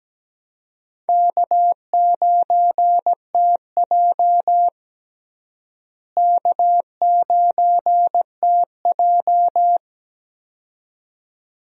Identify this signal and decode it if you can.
Morse code, heard as a keyed tone, about 17 wpm, 700 Hz. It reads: K9TJ K9TJ
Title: USA Call Signs - 4 Characters - Repeated 2x - 17wpm